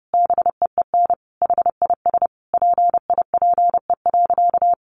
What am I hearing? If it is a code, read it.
Morse code, heard as a keyed tone, about 30 wpm, 700 Hz. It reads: BEEN HIS PIPE.
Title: Sentences from Top 1500 Words 30wpm